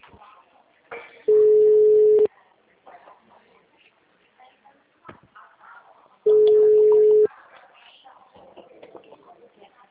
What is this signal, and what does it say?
Ringback tone (Europe, 425 Hz), with speech.
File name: out- -2008